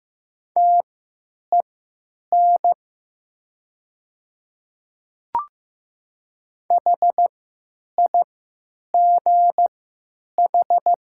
Morse code, heard as a keyed tone, about 15 wpm, spaced as for 9 wpm, 700 Hz. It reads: TEN HIGH
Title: 3x Character Spacing - Top 500 Words - 15wpm